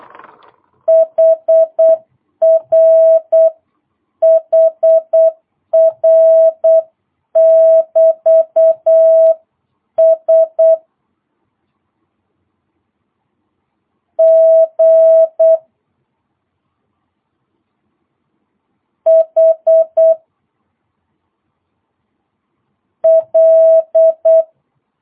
Morse code, heard as a keyed tone, about 8 wpm, 650 Hz. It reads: HR HR <BT> S G H L